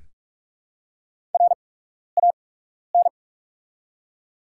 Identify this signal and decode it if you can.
Morse code, heard as a keyed tone, about 45 wpm, spaced as for 12 wpm, 690 Hz. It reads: RAN
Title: 8x Character Spacing - Top 500 Words - 45wpm